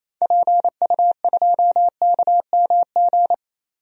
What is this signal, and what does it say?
Morse code, heard as a keyed tone, about 28 wpm, 700 Hz. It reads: PU2KMG
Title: DXCC-100 Call Signs - 6 Characters - 28wpm